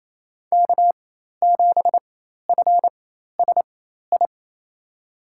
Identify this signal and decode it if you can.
Morse code, heard as a keyed tone, about 28 wpm, spaced as for 13 wpm, 700 Hz. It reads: K7FSI
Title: USA Call Signs - 5 Characters - 4x Character Spacing - 28wpm